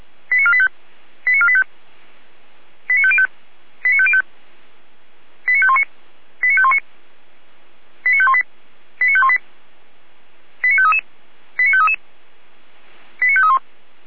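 Five-tone selective call (ZVEI), heard as quick runs of five tones, tones of about 70 ms. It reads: 89485 89485 89505 89505 89519 89519 89518 89518 89533 89533 89531
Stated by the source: ZVEI five-tone call received by SDR.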